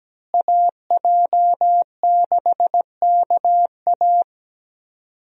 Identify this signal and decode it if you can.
Morse code, heard as a keyed tone, about 17 wpm, 700 Hz. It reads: AJ6KA
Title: USA Call Signs - 5 Characters - Repeated 2x - 17wpm